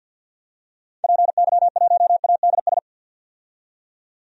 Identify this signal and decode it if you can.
Morse code, heard as a keyed tone, about 50 wpm, 700 Hz. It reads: PY1ADS
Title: DXCC-100 Call Signs - 6 Characters - 50wpm